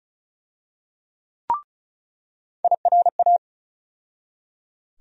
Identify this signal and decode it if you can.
Morse code, heard as a keyed tone, about 36 wpm, 700 Hz. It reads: IRA
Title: USA Call Sign Postfixes - 3 Letters 35wpm